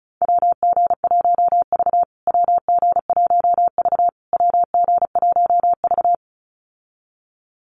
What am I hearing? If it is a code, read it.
Morse code, heard as a keyed tone, about 34 wpm, 700 Hz. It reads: WG1V WG1V WG1V
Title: USA Call Signs Repeated 3x 35wpm